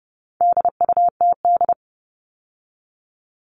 Morse code, heard as a keyed tone, about 30 wpm, 700 Hz. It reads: DUTD
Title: All Cut Numbers - 4-Digit 30wpm